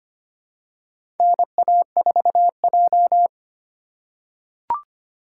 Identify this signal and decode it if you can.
Morse code, heard as a keyed tone, about 25 wpm, 700 Hz. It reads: NA4J
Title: USA Call Signs - 4x Character Spacing - 25wpm